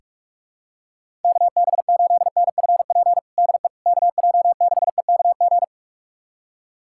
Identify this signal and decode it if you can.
Morse code, heard as a keyed tone, about 46 wpm, 700 Hz. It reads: KB8NFP DE KJ6EKG
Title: USA Callsign DE Callsign - 6 Characters - 45wpm